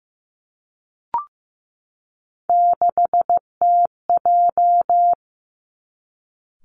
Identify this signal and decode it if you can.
Morse code, heard as a keyed tone, about 15 wpm, 700 Hz. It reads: 6TJ
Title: Three Challenging Letter-Number Combinations 15wpm